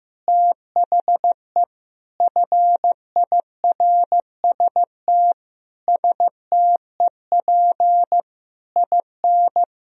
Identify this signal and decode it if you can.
Morse code, heard as a keyed tone, about 15 wpm, 700 Hz. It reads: THE FIRST STEP IN
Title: Sentences from Top 1000 Words - Review Part 4 - 15wpm